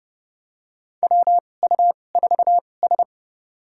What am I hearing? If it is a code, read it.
Morse code, heard as a keyed tone, about 31 wpm, spaced as for 22 wpm, 700 Hz. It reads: WU4S
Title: USA Call Signs - 4 Characters - 2x Character Spacing - 30wpm